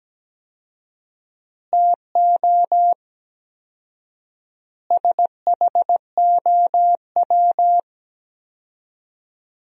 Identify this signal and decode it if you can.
Morse code, heard as a keyed tone, about 17 wpm, 700 Hz. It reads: TO SHOW
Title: 4x Word-Spacing - Sentences from Top 200 Words - 17wpm